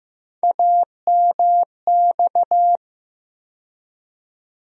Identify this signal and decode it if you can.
Morse code, heard as a keyed tone, about 15 wpm, 700 Hz. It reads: AMX
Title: USA Call Sign Postfixes - 3 Letters 15wpm